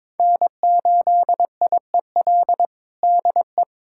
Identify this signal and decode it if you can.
Morse code, heard as a keyed tone, about 22 wpm, 700 Hz. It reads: N8IEL DE
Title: USA Callsign DE Callsign 22wpm